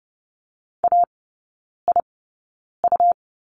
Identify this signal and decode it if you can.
Morse code, heard as a keyed tone, about 30 wpm, spaced as for 9 wpm, 690 Hz. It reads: AIU
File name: Lesson 17e - Letter u - Three Characters 30wpm